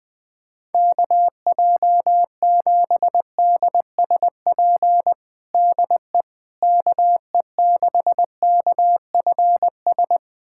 Morse code, heard as a keyed tone, about 20 wpm, 700 Hz. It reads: KJ7DSP DE KE6KFS
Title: USA Callsign DE Callsign - 6 Characters - 20wpm